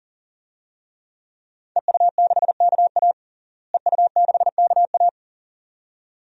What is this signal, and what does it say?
Morse code, heard as a keyed tone, about 40 wpm, 700 Hz. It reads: EU6KA EU6KA
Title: DXCC-100 Call Signs - 5 Characters - Repeated 2x - 40wpm